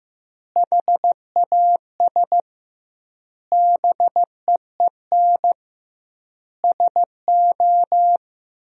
Morse code, heard as a keyed tone, about 15 wpm, 700 Hz. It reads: HAS BEEN SO
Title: 2x Word-Spacing - Sentences from Top 100 Words - 15wpm